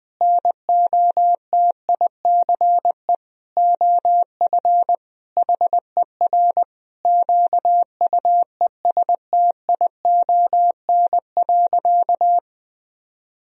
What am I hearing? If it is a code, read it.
Morse code, heard as a keyed tone, about 21 wpm, 700 Hz. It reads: NOTICE OF HER QUESTION.
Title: Sentences from Top 1000 Words - Review Part 2 - 20wpm